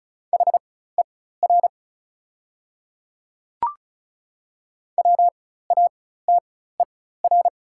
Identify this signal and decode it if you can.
Morse code, heard as a keyed tone, about 34 wpm, spaced as for 16 wpm, 700 Hz. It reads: HER WATER
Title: 4x Character Spacing - Top 100 Words - 35wpm